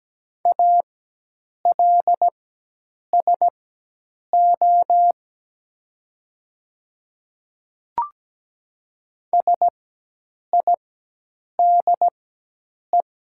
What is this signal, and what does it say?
Morse code, heard as a keyed tone, about 17 wpm, spaced as for 8 wpm, 700 Hz. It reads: ALSO SIDE
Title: 4x Character Spacing - Top 200 Words - 17wpm